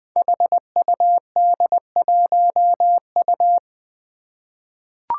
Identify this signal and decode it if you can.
Morse code, heard as a keyed tone, about 20 wpm, 700 Hz. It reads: HUD1U
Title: Five Challenging Letter-Number Combinations 20wpm